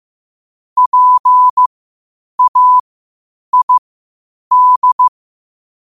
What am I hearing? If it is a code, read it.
Morse code, heard as a keyed tone, about 15 wpm, spaced as for 8 wpm, 1000 Hz. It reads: PAID
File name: qotdfiles / 8wpm